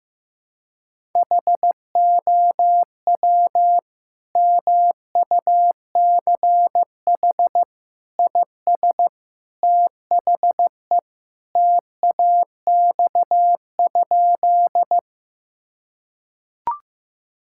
Morse code, heard as a keyed tone, about 15 wpm, 700 Hz. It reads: HOW MUCH IS THE TAX?
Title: Sentences from Top 1000 Words - Review Part 5 - 15wpm